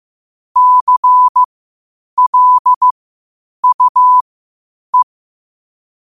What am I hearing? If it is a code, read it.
Morse code, heard as a keyed tone, about 15 wpm, spaced as for 8 wpm, 1000 Hz. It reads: CLUE
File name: qotdfiles / 8wpm